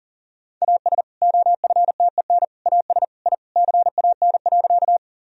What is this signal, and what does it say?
Morse code, heard as a keyed tone, about 40 wpm, 700 Hz. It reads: AS OFTEN AS I CAN.
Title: Sentences from Top 1000 Words - Review Part 1 - 40wpm